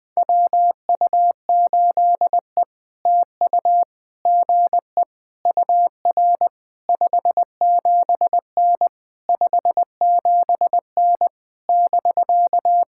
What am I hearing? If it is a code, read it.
Morse code, heard as a keyed tone, about 20 wpm, 700 Hz. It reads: WU8E TU GE UR 57N 57N <BK>
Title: POTA QSO Practice - 20wpm